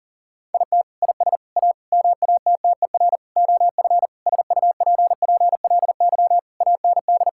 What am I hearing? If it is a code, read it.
Morse code, heard as a keyed tone, about 40 wpm, 700 Hz. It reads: IT IS A MATTER OF SUPPLY AND
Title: Sentences from Top 1000 Words - Review Part 5 - 40wpm